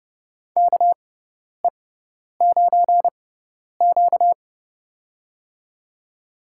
Morse code, heard as a keyed tone, about 31 wpm, spaced as for 10 wpm, 700 Hz. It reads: KE9Q
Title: USA Call Signs - 4 Characters - 6x Character Spacing - 30wpm